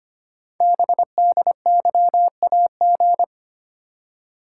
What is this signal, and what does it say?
Morse code, heard as a keyed tone, about 25 wpm, 700 Hz. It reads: BDYAG
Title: Code Groups - 5 Letters - 25wpm